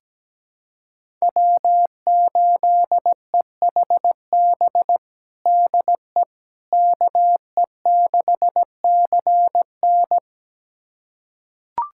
Morse code, heard as a keyed tone, about 17 wpm, 700 Hz. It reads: W8EHB DE KE6CN
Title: USA Callsign DE Callsign - 5 Characters - 17wpm